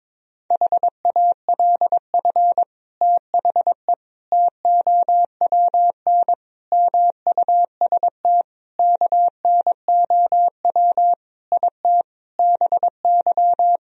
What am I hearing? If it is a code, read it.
Morse code, heard as a keyed tone, about 22 wpm, 700 Hz. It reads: HALF THE TOWN MUST KNOW IT BY
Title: Sentences from Top 1000 Words - Review Part 2 - 22wpm